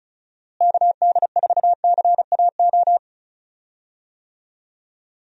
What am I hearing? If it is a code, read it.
Morse code, heard as a keyed tone, about 36 wpm, 700 Hz. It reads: KD4CAO
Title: USA Call Signs - 6 Characters - Repeated 2x - 35wpm